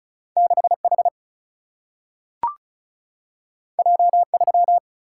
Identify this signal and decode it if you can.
Morse code, heard as a keyed tone, about 36 wpm, 700 Hz. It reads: 6H J3
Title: Single-Letter Single-Number 35wpm